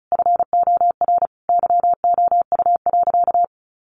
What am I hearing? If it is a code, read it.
Morse code, heard as a keyed tone, about 34 wpm, 700 Hz. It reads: FOR YOU.
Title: Sentences from Top 1500 Words 35wpm